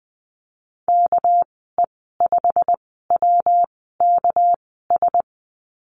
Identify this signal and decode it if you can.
Morse code, heard as a keyed tone, about 20 wpm, spaced as for 15 wpm, 700 Hz. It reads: KE5WKS
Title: 207 Lesson 33g - Letter j - Call Signs 20wpm